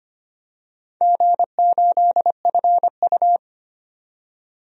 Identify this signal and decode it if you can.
Morse code, heard as a keyed tone, about 25 wpm, 690 Hz. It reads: G8FU
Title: DXCC-100 Call Signs - 4 Characters - 25wpm